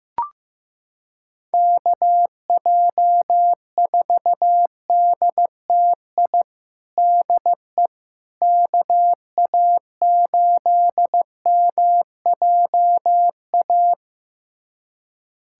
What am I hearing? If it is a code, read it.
Morse code, heard as a keyed tone, about 15 wpm, 700 Hz. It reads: KJ4DTI DE KA8MJA
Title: USA Callsign DE Callsign 15wpm